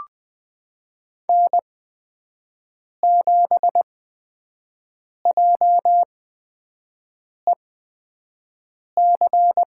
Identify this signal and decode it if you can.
Morse code, heard as a keyed tone, about 20 wpm, spaced as for 5 wpm, 700 Hz. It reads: N7JEC
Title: USA Call Signs - 5 Characters - 8x Character Spacing - 20wpm